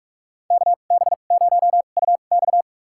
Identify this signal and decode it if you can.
Morse code, heard as a keyed tone, about 46 wpm, spaced as for 32 wpm, 700 Hz. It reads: KB0UX
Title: USA Call Signs - 5 Characters - 2x Character Spacing - 45wpm